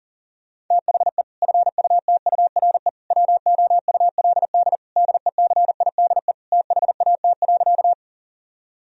Morse code, heard as a keyed tone, about 37 wpm, 700 Hz. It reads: THE FUTURE WOULD DECIDE THAT.
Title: Sentences from Top 1000 Words - Review Part 3 - 40wpm